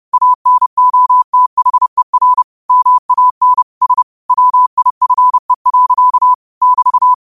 Morse code, heard as a keyed tone, about 30 wpm, 1000 Hz. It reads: ANOTHER MAN S WIFE. <BT>
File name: qotdfiles / 30wpm